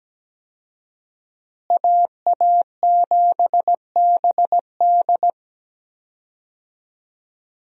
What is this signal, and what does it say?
Morse code, heard as a keyed tone, about 17 wpm, 700 Hz. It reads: AA7BD
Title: USA Call Signs - 5 Characters - Repeated 2x - 17wpm